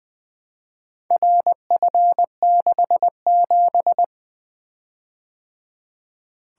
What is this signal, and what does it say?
Morse code, heard as a keyed tone, about 24 wpm, 700 Hz. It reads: RF67
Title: Speed-Racing - Four Letter-Number Combinations - 20wpm